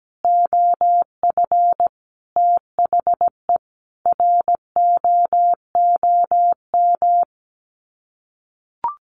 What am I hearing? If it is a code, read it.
Morse code, heard as a keyed tone, about 17 wpm, 700 Hz. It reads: OF THE ROOM
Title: Sets of 5 Words Encore2 17wpm